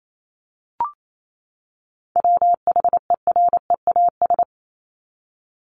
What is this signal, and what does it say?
Morse code, heard as a keyed tone, about 28 wpm, 700 Hz. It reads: WHEREAS
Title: Top 1501-1750 Words 28wpm